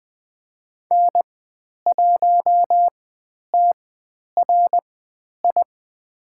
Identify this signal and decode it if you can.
Morse code, heard as a keyed tone, about 20 wpm, spaced as for 10 wpm, 700 Hz. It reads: N1TRI
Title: 109 Lesson 18g - Letter m - Call Signs 20wpm